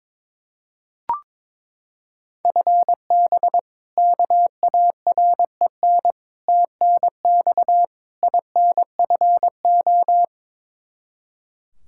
Morse code, heard as a keyed tone, about 21 wpm, 700 Hz. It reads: FB KAREN TNX INFO
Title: K1USN SST QSO Practice 22wpm